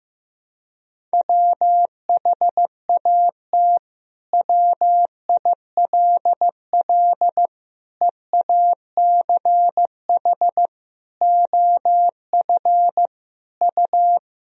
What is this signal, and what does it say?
Morse code, heard as a keyed tone, about 15 wpm, 700 Hz. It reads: WHAT WILL EACH OF U
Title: Sentences from Top 200 Words and Abbreviations 15wpm